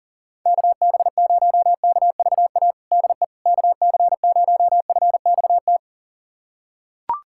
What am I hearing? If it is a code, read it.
Morse code, heard as a keyed tone, about 39 wpm, 700 Hz. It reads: KB0KVA DE KC0FXT
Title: USA Callsign DE Callsign - 6 Characters - 40wpm